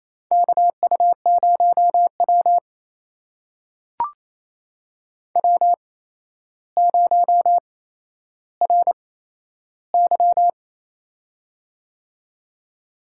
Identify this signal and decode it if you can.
Morse code, heard as a keyed tone, about 28 wpm, spaced as for 12 wpm, 700 Hz. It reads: KU0W W0RY
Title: USA Call Signs - 4 Characters - 8x Character Spacing - 28wpm